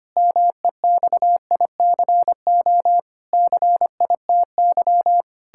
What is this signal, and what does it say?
Morse code, heard as a keyed tone, about 25 wpm, 700 Hz. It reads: MEXICO CITY
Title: Pangram Challenge v2 - 25wpm